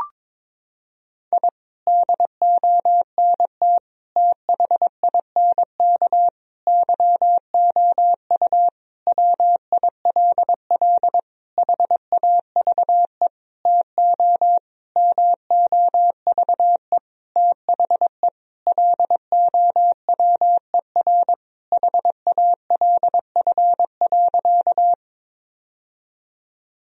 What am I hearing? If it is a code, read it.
Morse code, heard as a keyed tone, about 22 wpm, 700 Hz. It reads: I DONT THINK YOU WILL HAVE TO MOVE THE LOWER HALF.
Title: Sentences from Top 1000 Words - Review Part 4 - 22wpm